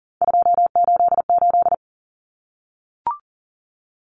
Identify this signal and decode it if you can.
Morse code, heard as a keyed tone, about 40 wpm, 700 Hz. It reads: 288
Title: Common Cut Numbers - 3-Digit 40wpm